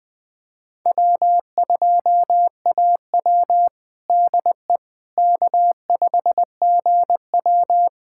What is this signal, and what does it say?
Morse code, heard as a keyed tone, about 20 wpm, 700 Hz. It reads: W2AW DE K5GW
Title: USA Callsign DE Callsign - 4 Characters - 20wpm